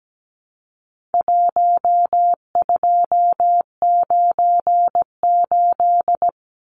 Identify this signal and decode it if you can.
Morse code, heard as a keyed tone, about 17 wpm, 700 Hz. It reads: 1298
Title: Numbers - 4-Digit 17wpm